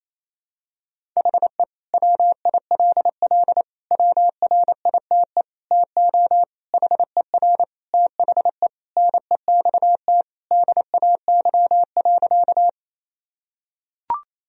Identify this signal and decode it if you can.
Morse code, heard as a keyed tone, about 28 wpm, 700 Hz. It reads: HE WILL WRITE TO HER THE NEXT DAY.
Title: Sentences from Top 1000 Words - Review Part 1 - 28wpm